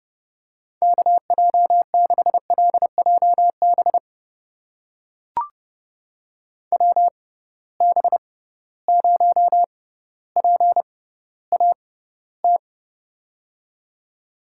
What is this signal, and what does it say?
Morse code, heard as a keyed tone, about 30 wpm, spaced as for 15 wpm, 700 Hz. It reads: KJ6LJB WB0PAT
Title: USA Call Signs - 6 Characters - 6x Character Spacing - 30wpm